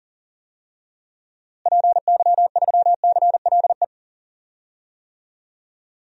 Morse code, heard as a keyed tone, about 38 wpm, 700 Hz. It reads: PY3CLE
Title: DXCC-100 Call Signs - 6 Characters - Repeated 2x - 40wpm